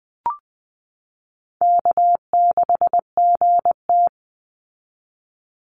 Morse code, heard as a keyed tone, about 20 wpm, 700 Hz. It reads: K6GT
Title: Call Signs Repeated 3x - 20wpm